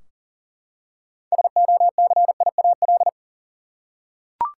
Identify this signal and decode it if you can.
Morse code, heard as a keyed tone, about 40 wpm, 700 Hz. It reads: SOCIAL
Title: Practice QSO - Exchange 3 - More Info - 40wpm